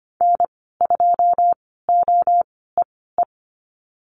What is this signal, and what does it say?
Morse code, heard as a keyed tone, about 25 wpm, spaced as for 16 wpm, 700 Hz. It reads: N2OEE
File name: Lesson 15g - Number 5 - Call Signs 25wpm